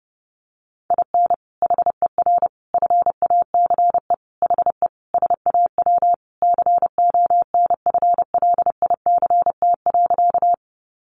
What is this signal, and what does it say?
Morse code, heard as a keyed tone, about 28 wpm, 700 Hz. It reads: IN HER FACE HE SAW CONFLICT.
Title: Sentences from Top 1300 Words 30wpm